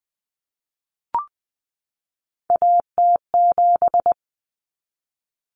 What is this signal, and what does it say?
Morse code, heard as a keyed tone, about 20 wpm, 700 Hz. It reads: AT7
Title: Common Cut Numbers - 3-Digit 20wpm